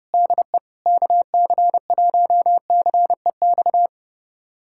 Morse code, heard as a keyed tone, about 32 wpm, 700 Hz. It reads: DE KC1CEX
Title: USA Callsign DE Callsign - 6 Characters - 30wpm